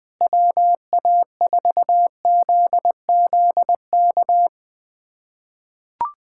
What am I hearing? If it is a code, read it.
Morse code, heard as a keyed tone, about 20 wpm, 700 Hz. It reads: WA4ZZK
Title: USA Callsign DE Callsign - 6 Characters - 20wpm